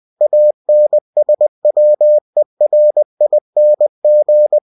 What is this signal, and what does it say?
Morse code, heard as a keyed tone, about 20 wpm, 600 Hz. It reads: ANSWERING
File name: winnie-20wpm